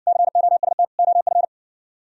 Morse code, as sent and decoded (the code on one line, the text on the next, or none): -.-. --- ... - / --- ..-.
COST OF